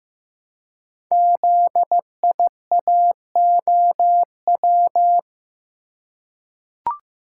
Morse code, heard as a keyed tone, about 15 wpm, 700 Hz. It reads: ZIAOW
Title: Code Groups - 5 Letter:Numbers - 15wpm